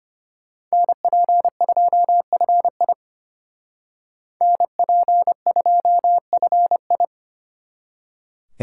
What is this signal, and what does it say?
Morse code, heard as a keyed tone, about 33 wpm, 700 Hz. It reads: NP2FI NP2FI